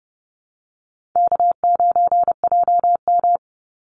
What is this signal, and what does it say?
Morse code, heard as a keyed tone, about 30 wpm, 700 Hz. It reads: K9JM
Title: Call Signs Repeated 2x - 30wpm